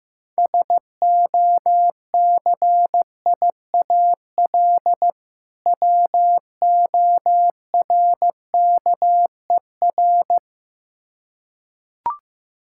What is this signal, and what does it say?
Morse code, heard as a keyed tone, about 15 wpm, 700 Hz. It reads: SOCIAL WORKER
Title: Practice QSO - Exchange 3 - More Info - 15wpm